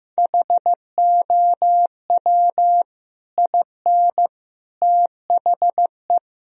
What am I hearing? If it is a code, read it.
Morse code, heard as a keyed tone, about 15 wpm, 700 Hz. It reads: HOW IN THE
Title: Sentences from Top 1000 Words - Review Part 4 - 15wpm